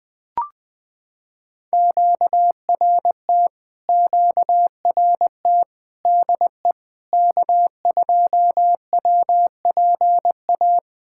Morse code, heard as a keyed tone, about 20 wpm, 700 Hz. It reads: QRT QRT DE K2WPA
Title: POTA QSO Practice - 20wpm